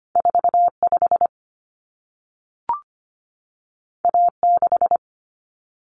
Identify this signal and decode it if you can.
Morse code, heard as a keyed tone, about 25 wpm, 700 Hz. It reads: 45 A6
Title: Common Cut Numbers - 2-Digit 25wpm